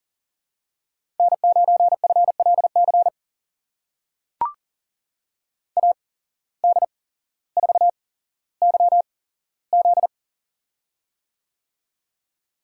Morse code, heard as a keyed tone, about 39 wpm, spaced as for 17 wpm, 700 Hz. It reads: N9FLC AD4YZ
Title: USA Call Signs - 5 Characters - 8x Character Spacing - 40wpm